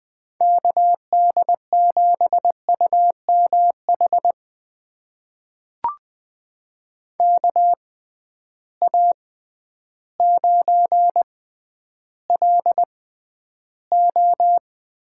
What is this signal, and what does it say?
Morse code, heard as a keyed tone, about 21 wpm, spaced as for 11 wpm, 700 Hz. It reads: KD7UMH KA9LO
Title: USA Call Signs - 6 Characters - 6x Character Spacing - 20wpm